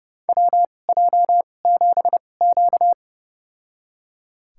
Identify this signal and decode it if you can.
Morse code, heard as a keyed tone, about 31 wpm, spaced as for 22 wpm, 700 Hz. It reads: WJ7Q
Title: USA Call Signs - 2x Character Spacing - 30wpm